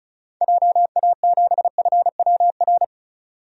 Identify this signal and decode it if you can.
Morse code, heard as a keyed tone, about 34 wpm, 700 Hz. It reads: JA7FWR
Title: DXCC-100 Call Signs - 6 Characters - 35wpm